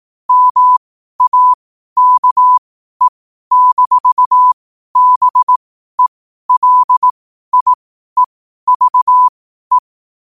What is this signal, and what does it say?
Morse code, heard as a keyed tone, about 18 wpm, spaced as for 12 wpm, 1000 Hz. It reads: MAKE-BELIEVE